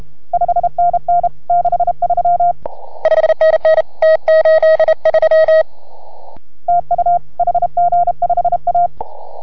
Morse code, heard as a keyed tone, about 31 wpm, 690 Hz. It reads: 5NN 63 5NN T83 TU HG5A